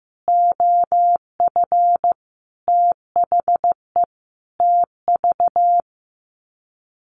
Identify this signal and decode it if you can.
Morse code, heard as a keyed tone, about 15 wpm, 700 Hz. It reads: OF THE TV
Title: Sets of 5 Words Encore2 15wpm